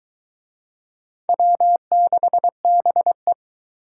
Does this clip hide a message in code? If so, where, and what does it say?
Yes: Morse code from 1.3 to 3.3 s, heard as a keyed tone, about 20 wpm, 700 Hz. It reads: W6BE